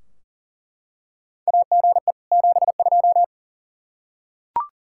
Morse code, heard as a keyed tone, about 37 wpm, 700 Hz. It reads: AGE 72
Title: Practice QSO - Exchange 3 - More Info - 40wpm